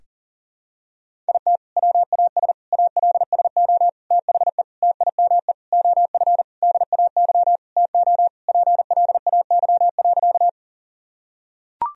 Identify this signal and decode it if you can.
Morse code, heard as a keyed tone, about 40 wpm, 700 Hz. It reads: IT WAS ALSO THE TIME OF DAY TO PLAY.
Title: Sentences from Top 1000 Words - Review Part 1 - 40wpm